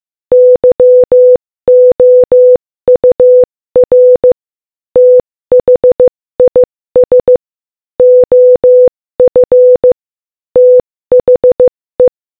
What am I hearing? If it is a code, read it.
Morse code, heard as a keyed tone, about 15 wpm, 500 Hz. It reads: YOUR THIS OF THE
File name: Morse-Code-Audio-Speed-15-9